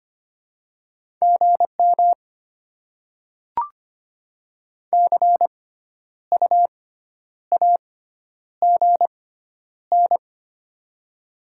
Morse code, heard as a keyed tone, about 25 wpm, spaced as for 10 wpm, 700 Hz. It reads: GM CU AGN
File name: Top 20 QSO Elements - 6x Character Spacing - 25wpm